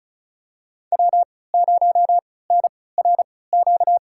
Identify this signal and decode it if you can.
Morse code, heard as a keyed tone, about 36 wpm, spaced as for 20 wpm, 700 Hz. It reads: W0NRQ
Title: USA Call Signs - 5 Characters - 3x Character Spacing - 35wpm